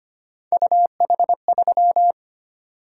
Morse code, heard as a keyed tone, about 25 wpm, 700 Hz. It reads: UH3
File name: Three Letter-Number Combinations - 25wpm